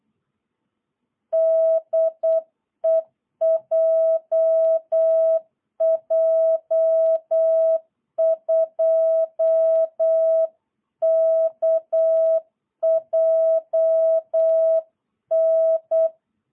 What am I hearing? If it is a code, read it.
Morse code, heard as a keyed tone, about 8 wpm, 650 Hz. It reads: DE JJ2KJN
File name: jh8cbh / 2022cw